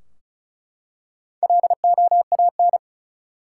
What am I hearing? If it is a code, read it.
Morse code, heard as a keyed tone, about 36 wpm, 700 Hz. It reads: LOAN
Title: Four Letter Words - 35wpm